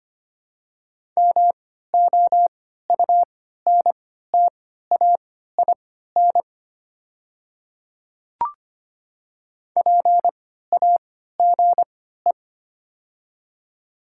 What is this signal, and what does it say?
Morse code, heard as a keyed tone, about 25 wpm, spaced as for 14 wpm, 700 Hz. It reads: MOUNTAIN PAGE
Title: 3x Character Spacing - Top 500 Words - 25wpm